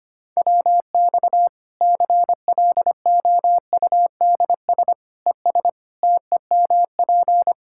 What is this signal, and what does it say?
Morse code, heard as a keyed tone, about 25 wpm, 700 Hz. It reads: WX CLOUDS ES TEMP
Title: Practice QSO - Exchange 2 - Info - 25wpm